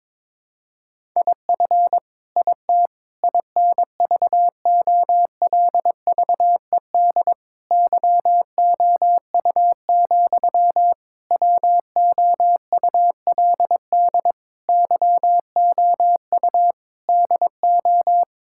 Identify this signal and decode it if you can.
Morse code, heard as a keyed tone, about 22 wpm, 700 Hz. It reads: IF IT INVOLVED YOU, WOULD YOU DO
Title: Sentences from Top 1000 Words - Review Part 5 - 22wpm